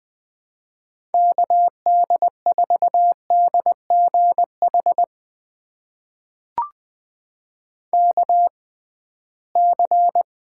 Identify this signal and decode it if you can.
Morse code, heard as a keyed tone, about 20 wpm, spaced as for 15 wpm, 700 Hz. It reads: KD4DGH KC
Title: USA Call Signs - 6 Characters - 6x Character Spacing - 20wpm